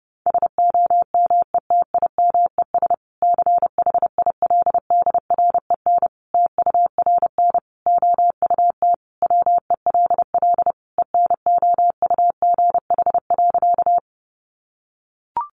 Morse code, heard as a keyed tone, about 30 wpm, 700 Hz. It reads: SOMETIMES CHILDREN TURN OUT WELL ENOUGH.
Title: Sentences from Top 600 Words 30wpm